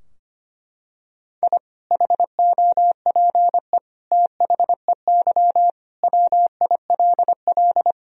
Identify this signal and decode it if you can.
Morse code, heard as a keyed tone, about 25 wpm, 700 Hz. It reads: I HOPE THEY WILL
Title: Sentences from Top 1000 Words - Review Part 2 - 25wpm